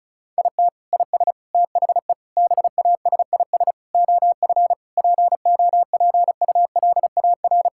Morse code, heard as a keyed tone, about 36 wpm, 700 Hz. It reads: IT IS THE BASIS OF POPULAR
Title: Sentences from Top 1000 Words - Review Part 5 - 35wpm